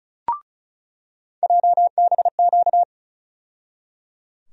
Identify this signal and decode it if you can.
Morse code, heard as a keyed tone, about 35 wpm, 700 Hz. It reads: JBQ
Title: USA Call Sign Postfixes - 3 Letters 35wpm